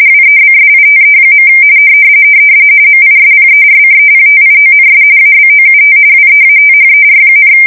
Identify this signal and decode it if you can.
RTTY, about 45.45 baud. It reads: WEBSITE. MISPERCEPTIONS ABOUT WEATHER,
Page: 170 Shift 2125/2295